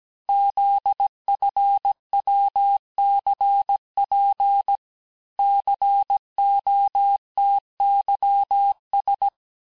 Morse code, heard as a keyed tone, about 17 wpm, 800 Hz. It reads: ZFWCP COTYS
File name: corrige-cours-1581-texte-chiffre-vitesse-16-mpm